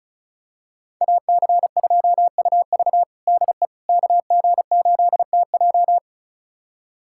Morse code, heard as a keyed tone, about 34 wpm, 700 Hz. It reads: AC2UV DE KG8TJ
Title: USA Callsign DE Callsign - 5 Characters - 35wpm